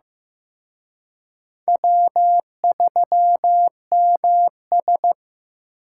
Morse code, heard as a keyed tone, about 18 wpm, 700 Hz. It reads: W3MS
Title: Speed Racing - USA 1x2 Call Signs 15wpm